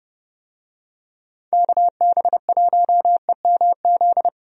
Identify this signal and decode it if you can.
Morse code, heard as a keyed tone, about 30 wpm, 700 Hz. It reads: KB1EMZ
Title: USA Call Signs - 6 Characters - 30wpm